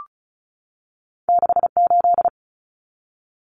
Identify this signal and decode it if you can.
Morse code, heard as a keyed tone, about 36 wpm, 700 Hz. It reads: 68
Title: Common Cut Numbers - 2-Digit 35wpm